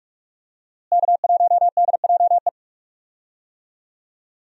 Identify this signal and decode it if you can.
Morse code, heard as a keyed tone, about 42 wpm, 700 Hz. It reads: K1DJE
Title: USA Call Signs - 5 Characters - Repeated 3x - 45wpm